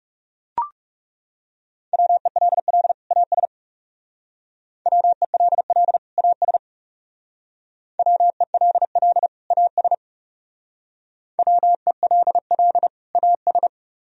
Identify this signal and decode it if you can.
Morse code, heard as a keyed tone, about 38 wpm, 700 Hz. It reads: WELL AS WELL AS WELL AS WELL AS